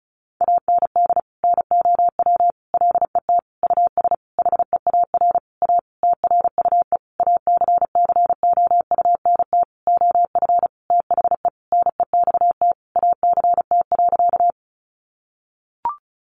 Morse code, heard as a keyed tone, about 34 wpm, 700 Hz. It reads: AND NOW LET US HEAR A TRUE ACCOUNT OF THE NEXT ACT.
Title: Sentences from Top 900 Words 35wpm